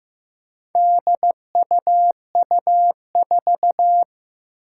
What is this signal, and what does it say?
Morse code, heard as a keyed tone, about 15 wpm, 700 Hz. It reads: DUU4
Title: Four Challenging Letter-Number Combinations 15wpm